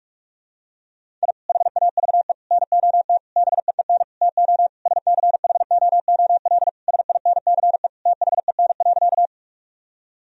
Morse code, heard as a keyed tone, about 46 wpm, 700 Hz. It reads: I HAVE NOT BEEN TO SCHOOL SINCE THEN.
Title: Sentences from Top 1000 Words - Review Part 1 - 45wpm